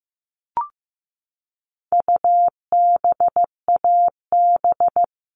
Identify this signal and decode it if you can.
Morse code, heard as a keyed tone, about 15 wpm, 700 Hz. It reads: UBAB
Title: All Cut Numbers - 4-Digit 15wpm